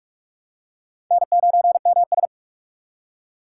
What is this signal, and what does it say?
Morse code, heard as a keyed tone, about 46 wpm, 700 Hz. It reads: N9MS
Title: K1USN SST QSO Practice - 45wpm